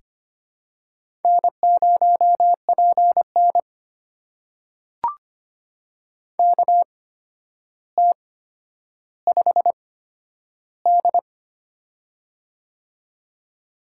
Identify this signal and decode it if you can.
Morse code, heard as a keyed tone, about 25 wpm, spaced as for 11 wpm, 700 Hz. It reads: N0PN KT5D
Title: USA Call Signs - 4 Characters - 8x Character Spacing - 25wpm